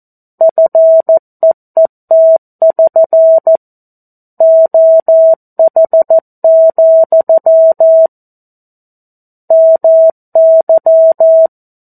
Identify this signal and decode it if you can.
Morse code, heard as a keyed tone, about 14 wpm, 650 Hz. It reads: FEET<SN> OH, MY
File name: 02-Chapter-02 14 WPM